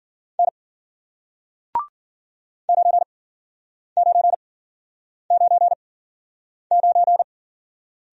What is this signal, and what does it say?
Morse code, heard as a keyed tone, about 53 wpm, spaced as for 9 wpm, 700 Hz. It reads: N 9999